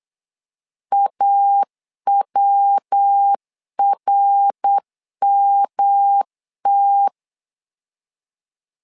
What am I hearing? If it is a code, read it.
Morse code, heard as a keyed tone, about 8 wpm, 800 Hz. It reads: AWRMT
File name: LEZIONI / 4 Q1 8wpm 3035